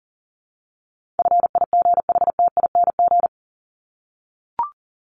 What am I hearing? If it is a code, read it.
Morse code, heard as a keyed tone, about 40 wpm, 700 Hz. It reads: FIGHTING
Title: ING Suffix - Top 100 Words with Base Words 40wpm